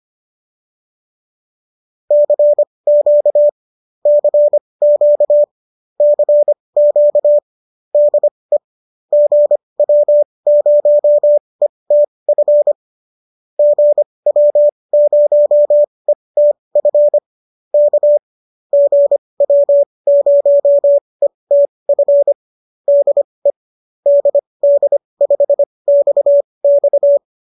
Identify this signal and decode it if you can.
Morse code, heard as a keyed tone, about 25 wpm, spaced as for 20 wpm, 590 Hz. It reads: CQ CQ CQ DE GW0ETF GW0ETF K GW0ETF DE DD5XX